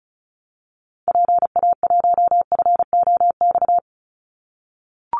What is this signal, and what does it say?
Morse code, heard as a keyed tone, about 34 wpm, 700 Hz. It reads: PA1FOX
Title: Call Signs - 35wpm